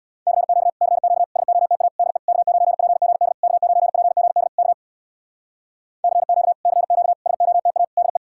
Morse code, heard as a keyed tone, about 65 wpm, 700 Hz. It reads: CQ CQ SOTA DE K9PGD K9PGD K CQ CQ SOTA DE